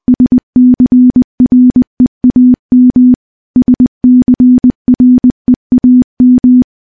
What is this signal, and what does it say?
Morse code, heard as a keyed tone, about 20 wpm, 250 Hz. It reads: SCREAM SCREAM
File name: morse-training-20WPM